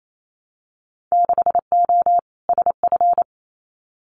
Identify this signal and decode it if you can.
Morse code, heard as a keyed tone, about 28 wpm, 700 Hz. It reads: 6O SF
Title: Winter Field Day Exchanges Encore 28wpm